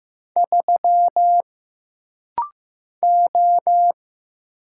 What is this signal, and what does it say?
Morse code, heard as a keyed tone, about 15 wpm, 700 Hz. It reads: 3 O
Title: Single Letter-Number Alt - Rapid-Fire 15wpm